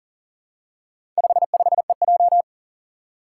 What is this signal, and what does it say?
Morse code, heard as a keyed tone, about 37 wpm, 700 Hz. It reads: 55EJ
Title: Four Challenging Letter-Number Combinations 40wpm